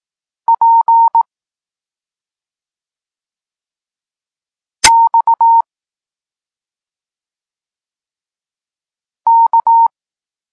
Morse code, heard as a keyed tone, about 18 wpm, 920 Hz. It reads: P X K
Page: Tekst 32 – Doskonalenie odbioru poznanych liter